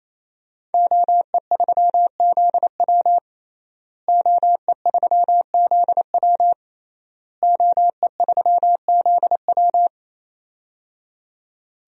Morse code, heard as a keyed tone, about 28 wpm, 700 Hz. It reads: OE3ZW OE3ZW OE3ZW
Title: DXCC-100 Call Signs - 5 Characters - Repeated 3x - 28wpm